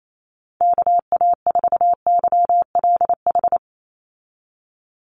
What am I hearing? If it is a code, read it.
Morse code, heard as a keyed tone, about 28 wpm, 700 Hz. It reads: KA4YLH
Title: USA Call Signs 28wpm